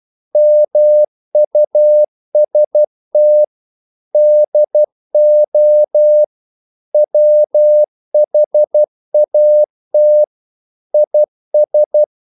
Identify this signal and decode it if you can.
Morse code, heard as a keyed tone, about 12 wpm, 600 Hz. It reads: MUST DO WHAT IS
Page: English quotations in CW (telegraphy - morse)